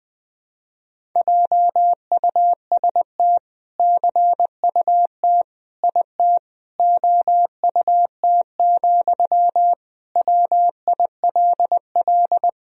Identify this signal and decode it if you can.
Morse code, heard as a keyed tone, about 20 wpm, 700 Hz. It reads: JUST CUT IT OUT, WILL
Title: Sentences from Top 300 Words and Abbreviations 20wpm